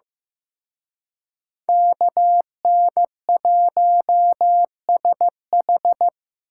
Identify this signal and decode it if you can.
Morse code, heard as a keyed tone, about 18 wpm, 700 Hz. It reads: KN1SH
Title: Speed Racing - USA 2x2 Call Signs 15wpm